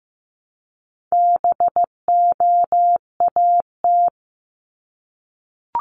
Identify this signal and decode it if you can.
Morse code, heard as a keyed tone, about 15 wpm, 700 Hz. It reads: BOAT
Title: Top 500 Words 15wpm